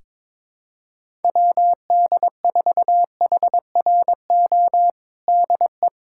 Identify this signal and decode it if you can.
Morse code, heard as a keyed tone, about 22 wpm, 700 Hz. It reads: WD4HRO DE
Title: USA Callsign DE Callsign 22wpm